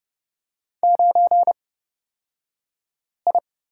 Morse code, heard as a keyed tone, about 30 wpm, spaced as for 5 wpm, 700 Hz. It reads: 9I
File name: Lesson 38d - Letter q - Two Characters 30wpm